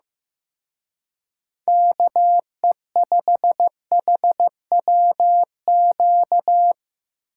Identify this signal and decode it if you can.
Morse code, heard as a keyed tone, about 17 wpm, 700 Hz. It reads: KE5HWQ
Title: Speed Racing - USA 2x3 Call Signs 15wpm